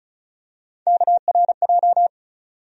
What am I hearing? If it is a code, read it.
Morse code, heard as a keyed tone, about 36 wpm, 700 Hz. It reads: KRJ
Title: USA Call Sign Postfixes - 3 Letters 35wpm